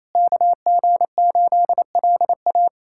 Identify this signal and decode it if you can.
Morse code, heard as a keyed tone, about 28 wpm, 700 Hz. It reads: KG8LA
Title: USA Callsign DE Callsign - 5 Characters - 28wpm